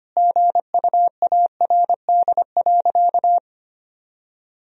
Morse code, heard as a keyed tone, about 25 wpm, 700 Hz. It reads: GUARD.
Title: Sentences from Top 1000 Words - Review Part 4 - 25wpm